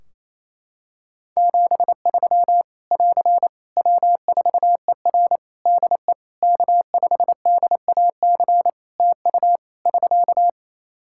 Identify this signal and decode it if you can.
Morse code, heard as a keyed tone, about 28 wpm, 700 Hz. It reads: 73 <AR> W4ER DE K5DAC TU <SK>
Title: K1USN SST QSO Practice 28wpm